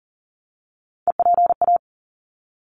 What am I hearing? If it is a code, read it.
Morse code, heard as a keyed tone, about 38 wpm, 700 Hz. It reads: EPA
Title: ARRL Sections 40wpm